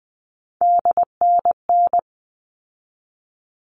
Morse code, heard as a keyed tone, about 20 wpm, 700 Hz. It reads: DNN
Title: All Cut Numbers - 3-Digit 20wpm